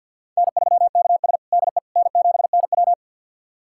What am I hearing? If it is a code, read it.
Morse code, heard as a keyed tone, about 50 wpm, 700 Hz. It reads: N3KS DE N7NW
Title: USA Callsign DE Callsign - 4 Characters - 50wpm